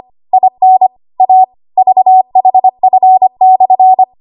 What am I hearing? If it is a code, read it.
Morse code, heard as a keyed tone, about 25 wpm, 750 Hz. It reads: IN A VHF/
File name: Archive / 25 WPM / 25WPM